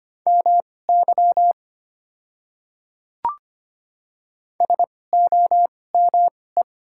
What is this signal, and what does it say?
Morse code, heard as a keyed tone, about 25 wpm, spaced as for 18 wpm, 700 Hz. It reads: MY SOME
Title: 2x Character Spacing - Top 200 Words - 25wpm